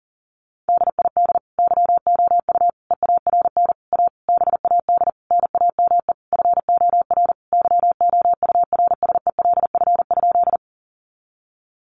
Morse code, heard as a keyed tone, about 40 wpm, 700 Hz. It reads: DID YOU EARN A BAD NAME FOR YOURSELF?
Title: Sentences from Top 1500 Words 40wpm